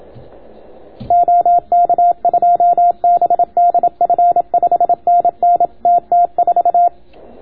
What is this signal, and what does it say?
Morse code, heard as a keyed tone, about 31 wpm, 690 Hz. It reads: OK2BDF 5NN TT4